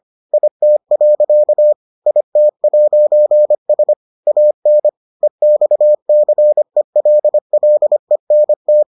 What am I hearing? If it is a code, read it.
Morse code, heard as a keyed tone, about 25 wpm, 600 Hz. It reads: IT. IT'S AN EXCELLENT